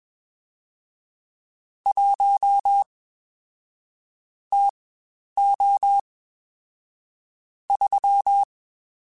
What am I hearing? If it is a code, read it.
Morse code, heard as a keyed tone, about 21 wpm, spaced as for 10 wpm, 800 Hz. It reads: 1 TO 3
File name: Spaced Characters / Spaced Characters-mp3s / Weather5 1 sp @20